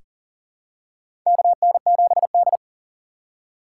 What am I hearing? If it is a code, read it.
Morse code, heard as a keyed tone, about 40 wpm, 700 Hz. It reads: KN7D
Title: K1USN SST QSO Practice - 40wpm